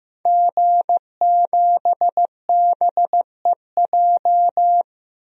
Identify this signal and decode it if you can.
Morse code, heard as a keyed tone, about 15 wpm, 700 Hz. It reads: G7BEJ
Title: DXCC-100 Call Signs - 5 Characters - Repeated 2x - 15wpm